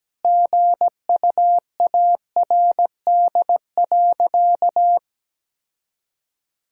Morse code, heard as a keyed tone, about 17 wpm, 700 Hz. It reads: GUARD.
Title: Sentences from Top 1000 Words - Review Part 4 - 17wpm